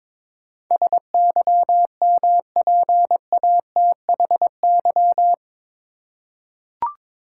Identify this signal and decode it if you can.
Morse code, heard as a keyed tone, about 22 wpm, 700 Hz. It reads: SYMPATHY
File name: practice / 22wpm / Eight Letter Words - 22wpm